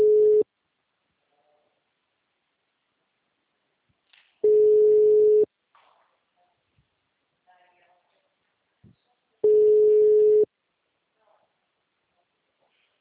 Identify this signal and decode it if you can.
Ringback tone (Europe, 425 Hz), with speech.
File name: out- -2026- - - .59890